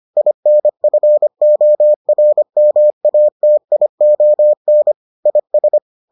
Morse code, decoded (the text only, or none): INFORMATION IS